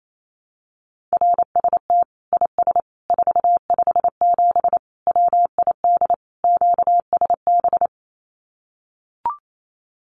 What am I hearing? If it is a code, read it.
Morse code, heard as a keyed tone, about 28 wpm, 700 Hz. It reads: RST IS 457 WID QSB
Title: Common QSO Phrases 28wpm